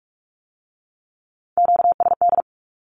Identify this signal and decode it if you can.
Morse code, heard as a keyed tone, about 46 wpm, 700 Hz. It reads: QSD
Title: Q-Codes with Spoken Meaning - 45wpm